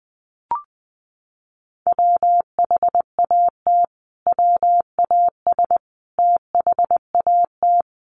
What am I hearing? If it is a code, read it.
Morse code, heard as a keyed tone, about 20 wpm, 700 Hz. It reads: WHAT WAS THAT
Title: Sentences from Top 1200 Words 20wpm